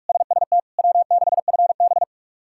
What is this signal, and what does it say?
Morse code, heard as a keyed tone, about 42 wpm, 700 Hz. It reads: SST W6FB
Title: K1USN SST QSO Practice - 45wpm